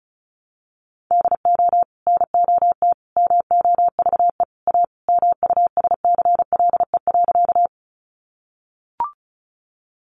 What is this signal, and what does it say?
Morse code, heard as a keyed tone, about 34 wpm, 700 Hz. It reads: DO NOT MOVE A MUSCLE.
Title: Sentences from Top 1300 Words 35wpm